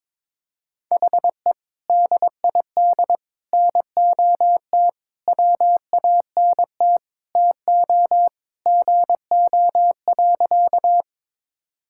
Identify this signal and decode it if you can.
Morse code, heard as a keyed tone, about 22 wpm, 700 Hz. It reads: HE DID NOT WANT TO GO.
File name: practice / 22wpm / Sentences with 3-Word Phrases - 22wpm